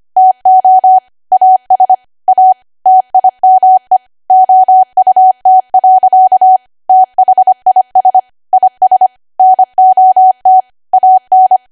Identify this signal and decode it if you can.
Morse code, heard as a keyed tone, about 25 wpm, 750 Hz. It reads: TO AS A TIME OUT. THIS IS NOT AN
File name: Archive / 25 WPM / 25WPM